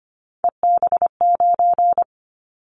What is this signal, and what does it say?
Morse code, heard as a keyed tone, about 25 wpm, 700 Hz. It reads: EB9
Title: Callsign Prefixes for Top 100 DXCC - 25wpm